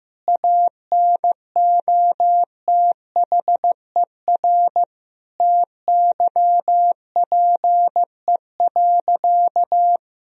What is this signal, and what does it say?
Morse code, heard as a keyed tone, about 15 wpm, 700 Hz. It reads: ANOTHER TYPE.
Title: Sentences from Top 1000 Words - Review Part 3 - 15wpm